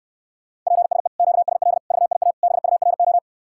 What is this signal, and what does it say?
Morse code, heard as a keyed tone, about 69 wpm, 700 Hz. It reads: PSE QSL VIA BURO